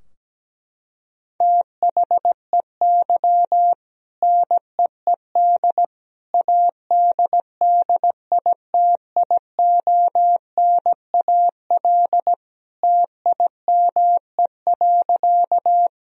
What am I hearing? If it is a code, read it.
Morse code, heard as a keyed tone, about 17 wpm, 700 Hz. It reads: THEY NEED ADDITIONAL TIME.
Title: Sentences from Top 1000 Words - Review Part 4 - 17wpm